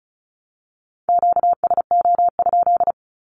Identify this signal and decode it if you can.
Morse code, heard as a keyed tone, about 34 wpm, 700 Hz. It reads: QSO?
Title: Q-Codes Repeated 3x 35wpm